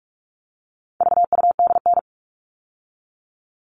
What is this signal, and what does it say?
Morse code, heard as a keyed tone, about 44 wpm, 700 Hz. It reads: VUDN
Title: All Cut Numbers - 4-Digit 45wpm